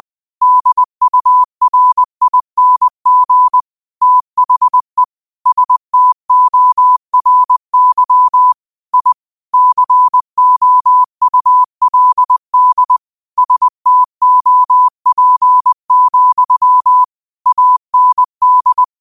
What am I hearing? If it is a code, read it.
Morse code, heard as a keyed tone, about 20 wpm, 1000 Hz. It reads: DURING THE STORY I COULD STOP, AND
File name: qotdfiles / 20wpm